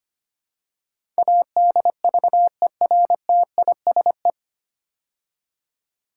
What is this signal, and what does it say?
Morse code, heard as a keyed tone, about 25 wpm, 700 Hz. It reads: ADVERTISE
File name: practice / 25wpm / 9 Letter Words - First 3 Letters Given - 25wpm